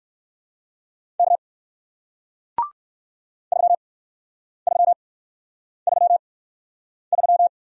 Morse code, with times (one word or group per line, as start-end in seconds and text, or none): 1.2-1.4 K
3.5-7.5 3333